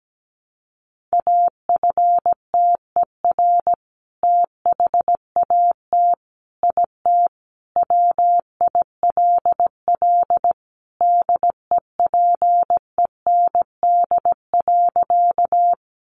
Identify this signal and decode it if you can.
Morse code, heard as a keyed tone, about 17 wpm, 700 Hz. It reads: AFTER THAT IT WILL DEPEND.
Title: Sentences from Top 1300 Words 17wpm